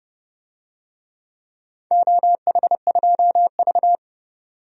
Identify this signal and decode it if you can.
Morse code, heard as a keyed tone, about 31 wpm, 700 Hz. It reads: OH2V
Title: DXCC-100 Call Signs - 4 Characters - 30wpm